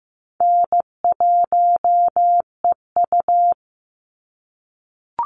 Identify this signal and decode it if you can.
Morse code, heard as a keyed tone, about 15 wpm, 700 Hz. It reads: N1EU
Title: NA SOTA Chaser Callsigns 15wpm